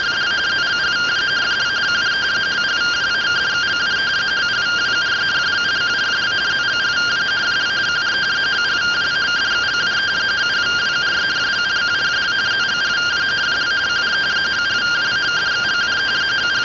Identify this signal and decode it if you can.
Two-tone teleprinter signal, about 170 Hz shift.